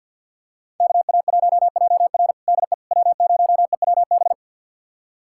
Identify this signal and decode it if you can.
Morse code, heard as a keyed tone, about 46 wpm, 700 Hz. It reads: KA1JR DE W0EWB
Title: USA Callsign DE Callsign - 5 Characters - 50wpm